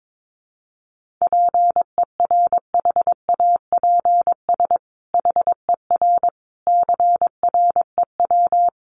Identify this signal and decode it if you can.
Morse code, heard as a keyed tone, about 22 wpm, 700 Hz. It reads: PERHAPS HER CREW
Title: Sentences from Top 1500 Words 22wpm